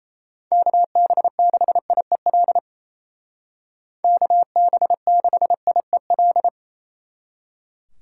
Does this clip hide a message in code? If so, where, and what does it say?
Yes: Morse code from 0.5 to 6.5 s, heard as a keyed tone, about 36 wpm, 700 Hz. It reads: KB6IEL KB6IEL